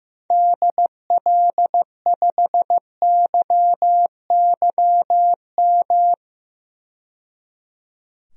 Morse code, heard as a keyed tone, about 15 wpm, 700 Hz. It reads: DL5YYM
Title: DXCC-100 Call Signs - 6 Characters - 15wpm